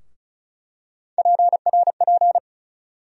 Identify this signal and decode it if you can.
Morse code, heard as a keyed tone, about 36 wpm, 700 Hz. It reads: PRP
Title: USA Call Sign Postfixes - 3 Letters 35wpm